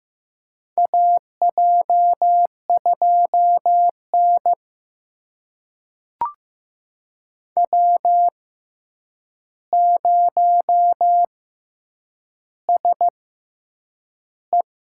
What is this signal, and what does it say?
Morse code, heard as a keyed tone, about 15 wpm, spaced as for 8 wpm, 700 Hz. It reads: AJ2N W0SE
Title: USA Call Signs - 4 Characters - 6x Character Spacing - 15wpm